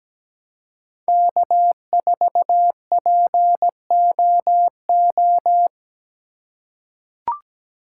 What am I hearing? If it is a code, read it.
Morse code, heard as a keyed tone, about 17 wpm, 700 Hz. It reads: K4POO
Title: USA Call Signs - 5 Characters - 2x Character Spacing - 17wpm